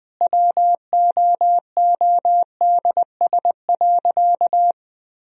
Morse code, heard as a keyed tone, about 21 wpm, 700 Hz. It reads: WOODS.
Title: Sentences from Top 1000 Words - Review Part 2 - 20wpm